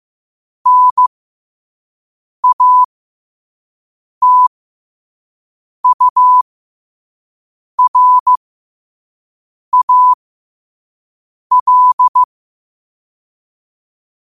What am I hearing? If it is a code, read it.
Morse code, heard as a keyed tone, about 15 wpm, spaced as for 5 wpm, 1000 Hz. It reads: NATURAL
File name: qotdfiles / 5wpm